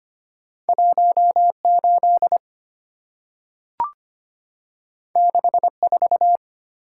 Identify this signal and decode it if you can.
Morse code, heard as a keyed tone, about 25 wpm, 700 Hz. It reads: 18 64
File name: Two Letter-Number Combinations - 25wpm